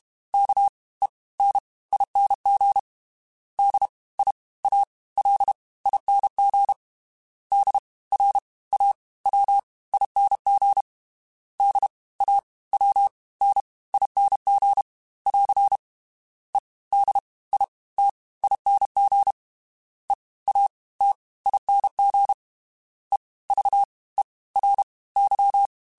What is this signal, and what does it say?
Morse code, heard as a keyed tone, about 32 wpm, 800 Hz. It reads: K E N ING D I A L ING D R A W ING D A W N ING <AR> E D I T ING E A T ING E V E R Y
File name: Special INGs / Special INGs-mp3s / Fast ing @30